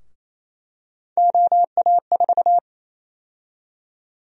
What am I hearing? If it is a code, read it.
Morse code, heard as a keyed tone, about 28 wpm, 700 Hz. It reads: OA4
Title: USA Call Sign Prefixes - Repeated 2x - 28wpm